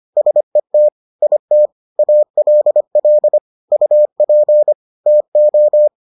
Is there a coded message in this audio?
Yes: Morse code from 0.2 to 5.9 s, heard as a keyed tone, about 25 wpm, 600 Hz. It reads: SET IT ALL UP TO